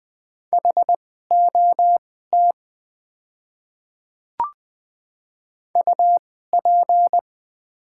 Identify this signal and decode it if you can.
Morse code, heard as a keyed tone, about 20 wpm, spaced as for 15 wpm, 700 Hz. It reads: HOT UP
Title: 2x Character Spacing - Top 100 Words - 20wpm